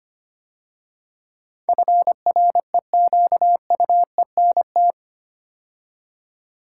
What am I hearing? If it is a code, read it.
Morse code, heard as a keyed tone, about 25 wpm, 700 Hz. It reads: FREQUENT
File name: 8 Letter Words - First 3 Letters Given - 25wpm